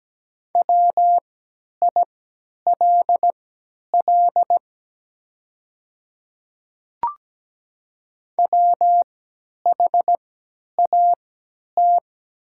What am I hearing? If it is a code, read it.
Morse code, heard as a keyed tone, about 17 wpm, spaced as for 10 wpm, 700 Hz. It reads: WILL WHAT
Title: 3x Character Spacing - Top 100 Words - 17wpm